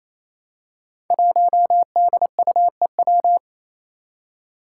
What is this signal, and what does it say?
Morse code, heard as a keyed tone, about 28 wpm, 700 Hz. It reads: 1DUEW
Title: Five Challenging Letter-Number Combinations 28wpm